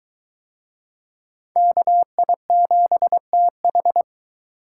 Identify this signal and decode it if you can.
Morse code, heard as a keyed tone, about 23 wpm, 700 Hz. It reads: KI7TH